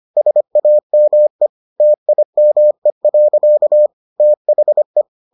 Morse code, heard as a keyed tone, about 25 wpm, 600 Hz. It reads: SAME TIME. THE